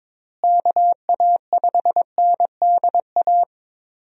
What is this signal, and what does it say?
Morse code, heard as a keyed tone, about 22 wpm, 700 Hz. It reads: KA5NDA
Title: USA Call Signs - 6 Characters - Repeated 3x - 22wpm